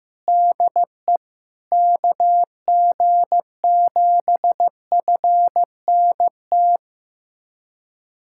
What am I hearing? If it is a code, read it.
Morse code, heard as a keyed tone, about 16 wpm, 700 Hz. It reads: DE KG7FNT
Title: USA Callsign DE Callsign - 6 Characters - 15wpm